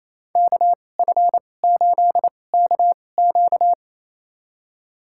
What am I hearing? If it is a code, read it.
Morse code, heard as a keyed tone, about 28 wpm, spaced as for 20 wpm, 700 Hz. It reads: KF8KQ
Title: USA Call Signs - 5 Characters - 2x Character Spacing - 28wpm